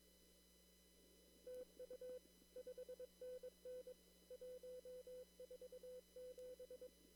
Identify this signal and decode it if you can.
Morse code, heard as a keyed tone, about 22 wpm, 500 Hz. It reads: TU 5NN 147
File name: CW / kt5j wpx cw-op-k5pi